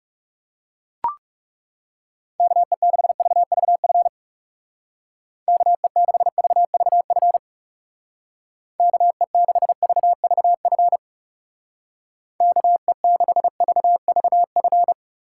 Morse code, heard as a keyed tone, about 37 wpm, 700 Hz. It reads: KE6VVF KE6VVF KE6VVF KE6VVF
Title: Speed Racing - USA 2x3 Call Signs 30wpm